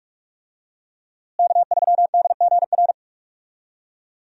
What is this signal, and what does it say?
Morse code, heard as a keyed tone, about 43 wpm, 700 Hz. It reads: K3DGR
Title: USA Call Signs - 5 Characters - Repeated 3x - 45wpm